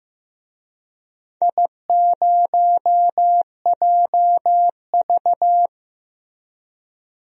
Morse code, heard as a keyed tone, about 16 wpm, 700 Hz. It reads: I0JV
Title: DXCC-100 Call Signs - 4 Characters - Repeated 2x - 15wpm